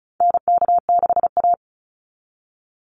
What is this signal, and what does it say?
Morse code, heard as a keyed tone, about 34 wpm, 700 Hz. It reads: NK6A
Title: NA SOTA Chaser Callsigns 35wpm